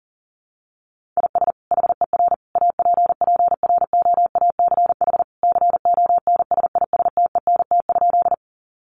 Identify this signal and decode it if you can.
Morse code, heard as a keyed tone, about 37 wpm, 700 Hz. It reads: IS HER APPROACH CONSISTENT?
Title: Sentences from Top 1500 Words 40wpm